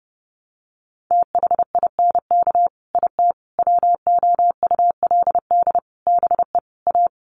Morse code, heard as a keyed tone, about 30 wpm, 700 Hz. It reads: THINK IT WOULD BE A
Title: Sets of 5 Words Encore2 30wpm